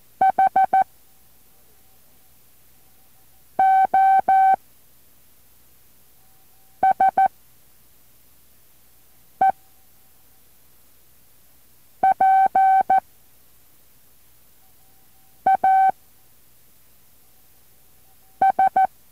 Morse code, heard as a keyed tone, about 14 wpm, 770 Hz. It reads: H O S E P A S